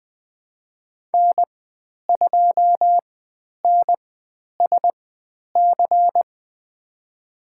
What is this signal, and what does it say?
Morse code, heard as a keyed tone, about 20 wpm, spaced as for 10 wpm, 700 Hz. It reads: N2NSC
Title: 102 Lesson 17g - Letter u - Call Signs 20wpm